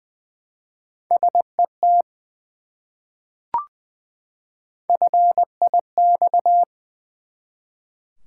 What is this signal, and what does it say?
Morse code, heard as a keyed tone, about 20 wpm, 700 Hz. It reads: SET FIX
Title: Three Letter Words - 20wpm